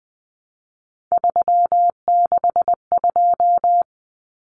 Morse code, heard as a keyed tone, about 20 wpm, 700 Hz. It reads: 362
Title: Common Cut Numbers - 3-Digit 20wpm